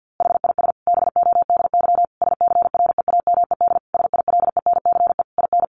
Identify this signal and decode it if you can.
Morse code, heard as a keyed tone, about 50 wpm, 700 Hz. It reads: HIS BODY SCREAMED SILENCE IN
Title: A Sound Of Thunder 50wpm